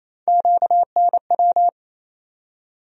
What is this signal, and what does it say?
Morse code, heard as a keyed tone, about 28 wpm, 700 Hz. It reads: QNW
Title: USA Call Sign Postfixes - 3 Letters 28wpm